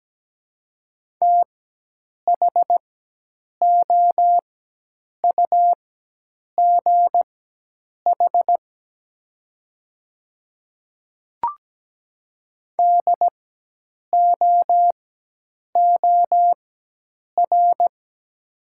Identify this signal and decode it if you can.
Morse code, heard as a keyed tone, about 17 wpm, spaced as for 8 wpm, 700 Hz. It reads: THOUGH DOOR
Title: 4x Character Spacing - Top 500 Words - 17wpm